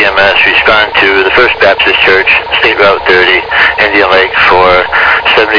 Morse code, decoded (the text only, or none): none